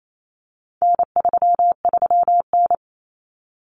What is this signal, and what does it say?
Morse code, heard as a keyed tone, about 28 wpm, 700 Hz. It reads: N33N
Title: Common Cut Numbers - 4-Digit 28wpm